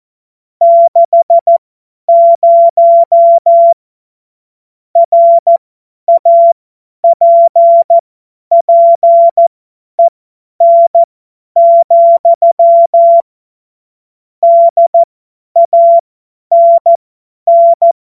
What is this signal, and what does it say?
Morse code, heard as a keyed tone, about 14 wpm, spaced as for 10 wpm, 680 Hz. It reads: 60 RAPPEN, DANN
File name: Witz-23-September-2025-50-F70